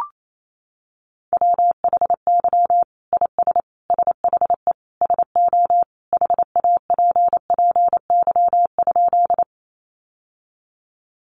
Morse code, heard as a keyed tone, about 28 wpm, 700 Hz. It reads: WHY IS SHE SO HAPPY?
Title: Sentences from Top 1000 Words 28wpm